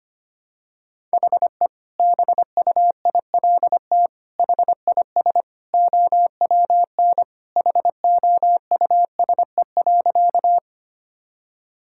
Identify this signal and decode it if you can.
Morse code, heard as a keyed tone, about 25 wpm, 700 Hz. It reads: HE BUILT HIS OWN HOUSE.
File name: practice / 25wpm / Sentences with 2-Word Phrases - 25wpm